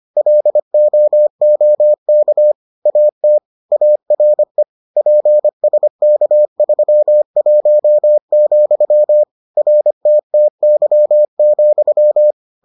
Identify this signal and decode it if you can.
Morse code, heard as a keyed tone, about 25 wpm, 600 Hz. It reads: LOOK AT ARE PSK31, RTTY,